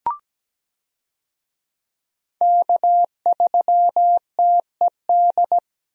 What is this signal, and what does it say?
Morse code, heard as a keyed tone, about 17 wpm, 700 Hz. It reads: K3TED
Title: USA Call Signs - 5 Characters - Repeated 3x - 17wpm